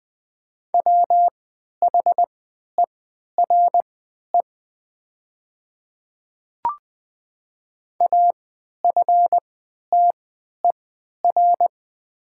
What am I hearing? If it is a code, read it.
Morse code, heard as a keyed tone, about 20 wpm, spaced as for 11 wpm, 690 Hz. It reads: WHERE AFTER
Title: 3x Character Spacing - Top 200 Words - 20wpm